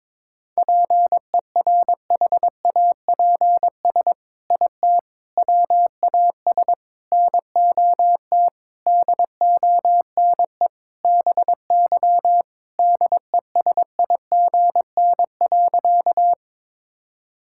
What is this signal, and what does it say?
Morse code, heard as a keyed tone, about 22 wpm, 700 Hz. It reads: PERHAPS IT WAS NOT DONE BY DESIGN.
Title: Sentences from Top 1000 Words - Review Part 5 - 22wpm